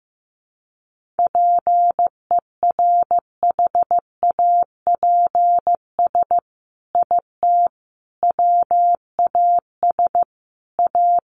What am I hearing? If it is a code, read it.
Morse code, heard as a keyed tone, about 15 wpm, 700 Hz. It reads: PERHAPS IT WAS A
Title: Sentences from Top 1500 Words 15wpm